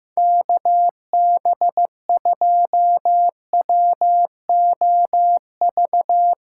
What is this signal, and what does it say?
Morse code, heard as a keyed tone, about 15 wpm, 700 Hz. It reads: KB2WOV
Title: USA Callsign DE Callsign - 6 Characters - 15wpm